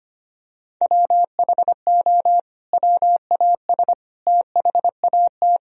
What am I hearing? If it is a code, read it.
Morse code, heard as a keyed tone, about 25 wpm, 700 Hz. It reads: WHO WAS THAT
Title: Sentences from Top 1000 Words - Review Part 4 - 25wpm